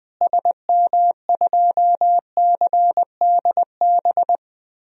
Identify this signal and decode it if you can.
Morse code, heard as a keyed tone, about 20 wpm, 700 Hz. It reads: SM2CDB
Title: DXCC-100 Call Signs - 6 Characters - Repeated 2x - 20wpm